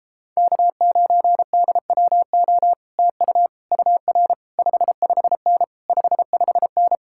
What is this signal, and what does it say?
Morse code, heard as a keyed tone, about 37 wpm, 700 Hz. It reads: K9DWO TU UR 55N 55N